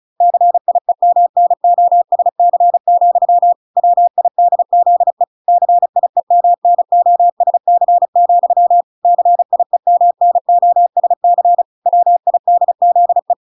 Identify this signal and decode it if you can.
Morse code, heard as a keyed tone, about 34 wpm, 700 Hz. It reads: CIEMNOSC, WIDZE CIEMNOSC, CIEMNOSC WIDZE